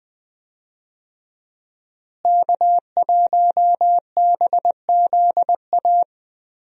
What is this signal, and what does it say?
Morse code, heard as a keyed tone, about 20 wpm, 700 Hz. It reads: K1BZA
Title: USA Call Signs - 5 Characters - Repeated 3x - 20wpm